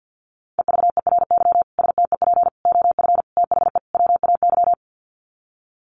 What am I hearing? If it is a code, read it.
Morse code, heard as a keyed tone, about 50 wpm, 700 Hz. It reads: EVERY STEP OF THE WAY
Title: Sets of 5 Words Encore2 50wpm